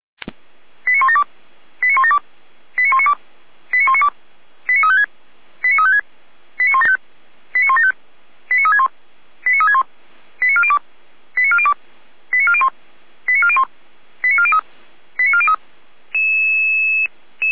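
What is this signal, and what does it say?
Five-tone selective call (ZVEI), heard as quick runs of five tones, tones of about 80 ms. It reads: 89172 89172 89192 89192 89356 89356 89175 89175 89361 89361 89402 89402 89401 89401 89403 89403
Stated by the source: ZVEI five-tone selective call received on an SDR.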